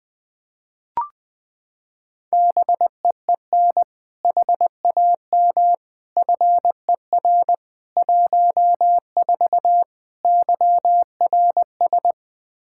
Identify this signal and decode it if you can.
Morse code, heard as a keyed tone, about 20 wpm, 700 Hz. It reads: BEEN HAM FER 14 YRS
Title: Practice QSO - Exchange 3 - More Info - 20wpm